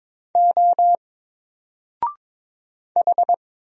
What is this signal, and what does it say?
Morse code, heard as a keyed tone, about 22 wpm, 700 Hz. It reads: O H
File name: Single-Letter-Number-22wpm